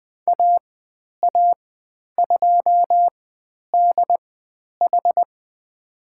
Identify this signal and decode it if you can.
Morse code, heard as a keyed tone, about 20 wpm, spaced as for 10 wpm, 700 Hz. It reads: AA2DH
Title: 109 Lesson 18g - Letter m - Call Signs 20wpm